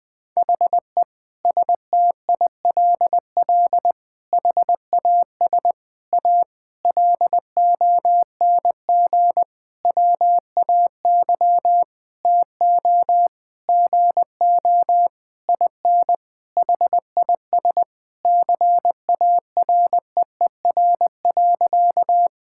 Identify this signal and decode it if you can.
Morse code, heard as a keyed tone, about 19 wpm, 700 Hz. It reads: HE STILL HAS A LONG WAY TO GO IN HIS CAREER.
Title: Sentences with 7-Word Phrases - Spoken Beginning - 20wpm